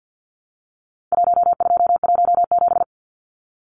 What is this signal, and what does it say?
Morse code, heard as a keyed tone, about 50 wpm, 700 Hz. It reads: 1217
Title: Numbers - Contest Style - 4-Digit 50wpm